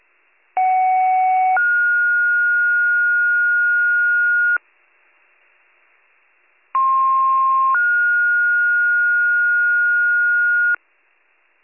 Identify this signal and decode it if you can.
Two-tone page, heard as a one-second tone, then a three-second tone, about 750 then 1430 Hz, about 1060 then 1430 Hz. It is heard from 0.6 to 10.7 s.